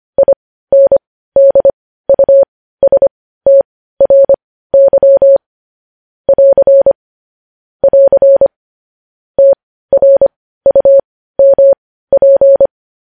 Morse code, heard as a keyed tone, about 25 wpm, spaced as for 15 wpm, 550 Hz. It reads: INDUSTRY <AR> <AR> TRUMP
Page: news headlines in morse code at 15 wpm